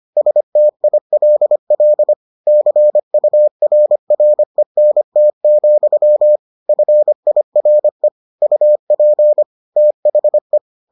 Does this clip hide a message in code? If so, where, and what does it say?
Yes: Morse code from 0.2 to 10.6 s, heard as a keyed tone, about 25 wpm, 600 Hz. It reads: STILL CURRENT, FIRE UP THE